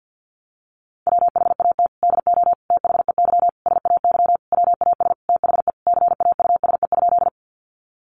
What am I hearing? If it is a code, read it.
Morse code, heard as a keyed tone, about 48 wpm, 700 Hz. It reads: WHAT DO THEY SAY WAS THE CAUSE?
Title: Sentences from Top 200 Words 50wpm